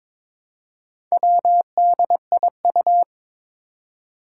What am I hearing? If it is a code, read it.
Morse code, heard as a keyed tone, about 23 wpm, 700 Hz. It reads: WDIU
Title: Four Challenging Letter-Number Combinations 22wpm